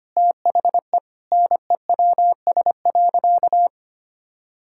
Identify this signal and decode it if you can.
Morse code, heard as a keyed tone, about 25 wpm, 700 Hz. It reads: THE NEWS.
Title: Sentences from Top 1000 Words - Review Part 6 - 25wpm